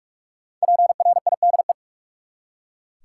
Morse code, heard as a keyed tone, about 44 wpm, 700 Hz. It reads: PRIDE
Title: Five Letter Words - 45wpm